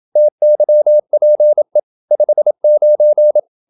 Morse code, heard as a keyed tone, about 27 wpm, 600 Hz. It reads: TYPE 59
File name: Never 0027wpm